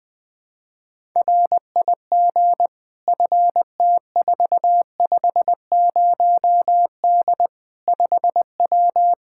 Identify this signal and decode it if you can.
Morse code, heard as a keyed tone, about 20 wpm, 700 Hz. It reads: RIG FT450D 5W
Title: Practice QSO - Exchange 2 - Info - 20wpm